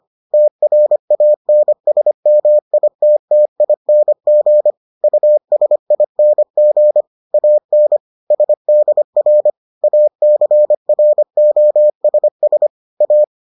Morse code, heard as a keyed tone, about 25 wpm, 600 Hz. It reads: TRANSMITTING USING AN SDR ACROSS A